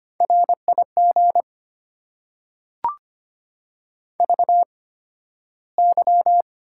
Morse code, heard as a keyed tone, about 25 wpm, spaced as for 13 wpm, 690 Hz. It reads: RIG VY
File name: Top 90 QSO Elements - 8x Character Spacing - 25wpm